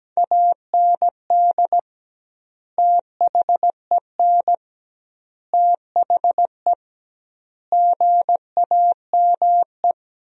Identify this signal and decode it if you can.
Morse code, heard as a keyed tone, about 17 wpm, 700 Hz. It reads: AND THEN THE GAME
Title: 2x Word-Spacing - Sentences from Top 500 Words - 17wpm